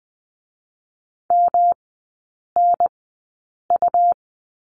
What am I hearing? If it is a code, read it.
Morse code, heard as a keyed tone, about 20 wpm, spaced as for 8 wpm, 700 Hz. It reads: MNU
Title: 107 Lesson 18e - Letter m - Three Characters 20wpm